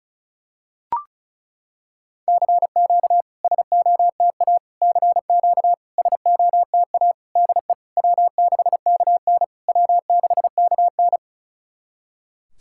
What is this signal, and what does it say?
Morse code, heard as a keyed tone, about 36 wpm, 690 Hz. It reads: CQ SOTA CQ SOTA DE W6KN W6KN
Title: SOTA QSO Practice 35wpm